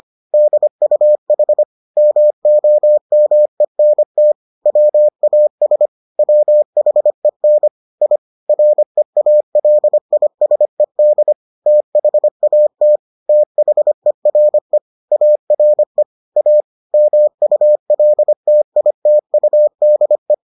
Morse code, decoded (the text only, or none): DUH MOMENT WAS WHEN I REALISED THAT THERE ARE A MULTITUDE